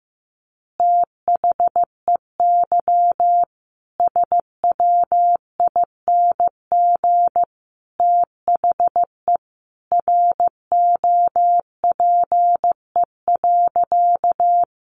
Morse code, heard as a keyed tone, about 15 wpm, 700 Hz. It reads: THEY SWING THE ROPE.
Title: Easy Phrases 15wpm